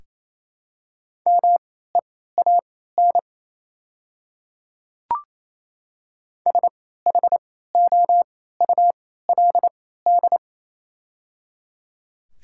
Morse code, heard as a keyed tone, about 28 wpm, spaced as for 16 wpm, 700 Hz. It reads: MEAN SHOULD
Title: 3x Character Spacing - Top 500 Words - 28wpm